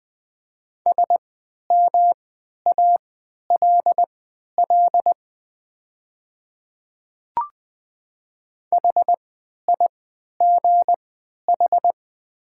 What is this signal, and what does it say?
Morse code, heard as a keyed tone, about 20 wpm, spaced as for 11 wpm, 690 Hz. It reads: SMALL HIGH
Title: 3x Character Spacing - Top 500 Words - 20wpm